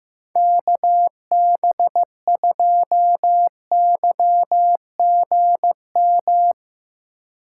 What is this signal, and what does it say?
Morse code, heard as a keyed tone, about 15 wpm, 700 Hz. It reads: KB2YGM
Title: USA Call Signs - 6 Characters - 2x Character Spacing - 15wpm